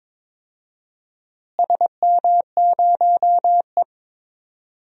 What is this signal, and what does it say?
Morse code, heard as a keyed tone, about 22 wpm, 700 Hz. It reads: SM0E
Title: DXCC-100 Call Signs - 4 Characters - Repeated 3x - 22wpm